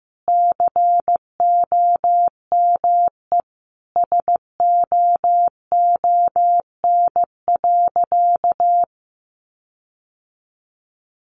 Morse code, heard as a keyed tone, about 15 wpm, 700 Hz. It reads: COME SOON.
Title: Sentences from Top 1500 Words 15wpm